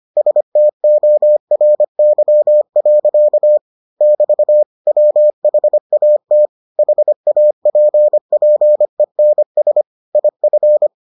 Morse code, heard as a keyed tone, about 25 wpm, 600 Hz. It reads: STORY. <BT> WHAT HAPPENS IF